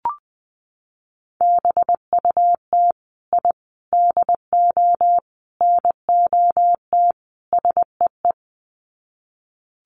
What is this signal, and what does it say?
Morse code, heard as a keyed tone, about 20 wpm, 700 Hz. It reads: BUT I DO NOT SEE
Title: Sets of 5 Words Encore 20wpm